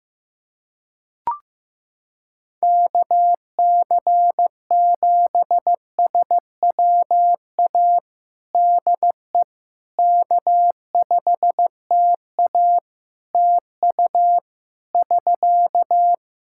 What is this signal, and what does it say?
Morse code, heard as a keyed tone, about 15 wpm, 700 Hz. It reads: KC7SWA DE K5TA TU <SK>
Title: Practice QSO - Exchange 4 - Ending - 15wpm